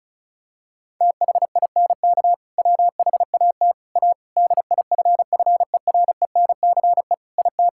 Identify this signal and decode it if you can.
Morse code, heard as a keyed tone, about 34 wpm, 700 Hz. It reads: THINK WHAT A DIFFERENCE IT
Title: Sentences from Top 1000 Words - Review Part 5 - 35wpm